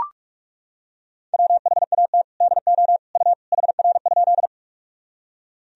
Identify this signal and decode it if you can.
Morse code, heard as a keyed tone, about 42 wpm, 700 Hz. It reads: WHAT DO U HR?
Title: Sentences from Top 300 Words and Abbreviations 45wpm